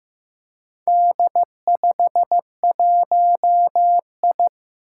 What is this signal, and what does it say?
Morse code, heard as a keyed tone, about 15 wpm, 700 Hz. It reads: D51I
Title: Four Challenging Letter-Number Combinations 15wpm